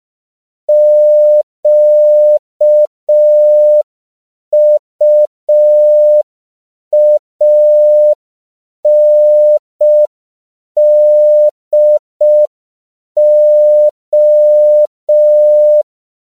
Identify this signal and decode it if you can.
Morse code, heard as a keyed tone, about 5 wpm, 600 Hz. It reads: QUANDO
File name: CW / 5PPM